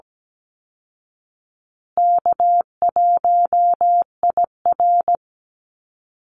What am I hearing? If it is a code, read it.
Morse code, heard as a keyed tone, about 17 wpm, 700 Hz. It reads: K1IR
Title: Easy Call Signs Repeated 3x - 17wpm